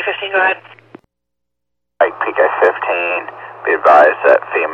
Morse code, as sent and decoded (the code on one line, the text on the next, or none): none